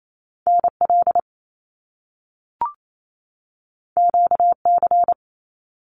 Morse code, heard as a keyed tone, about 28 wpm, 700 Hz. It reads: NL QC
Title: Canadian Provinces and Territories 28wpm